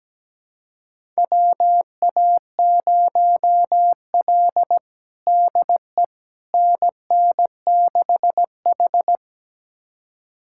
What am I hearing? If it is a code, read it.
Morse code, heard as a keyed tone, about 17 wpm, 700 Hz. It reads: WA0L DE NN6H
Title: USA Callsign DE Callsign - 4 Characters - 17wpm